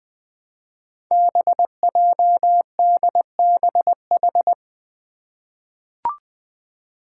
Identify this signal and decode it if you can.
Morse code, heard as a keyed tone, about 20 wpm, 700 Hz. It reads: BJDBH
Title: Five Challenging Letter-Number Combinations 20wpm